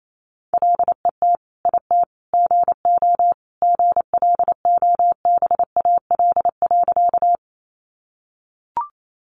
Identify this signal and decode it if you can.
Morse code, heard as a keyed tone, about 28 wpm, 700 Hz. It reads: LET IT GO GLOBAL.
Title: Sentences from Top 1200 Words 28wpm